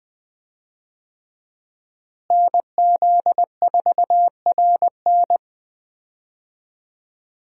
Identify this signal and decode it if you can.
Morse code, heard as a keyed tone, about 20 wpm, 700 Hz. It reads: NZ4RN
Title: USA Call Signs - 5 Characters - Repeated 2x - 20wpm